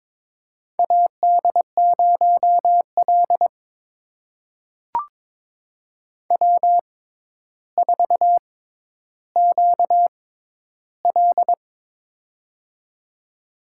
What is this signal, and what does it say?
Morse code, heard as a keyed tone, about 22 wpm, spaced as for 11 wpm, 700 Hz. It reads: AD0L W4QL
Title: USA Call Signs - 4 Characters - 6x Character Spacing - 22wpm